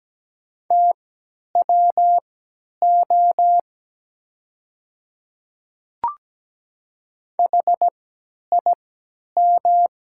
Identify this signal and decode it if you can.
Morse code, heard as a keyed tone, about 17 wpm, spaced as for 10 wpm, 700 Hz. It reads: TWO HIM
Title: 3x Character Spacing - Top 100 Words - 17wpm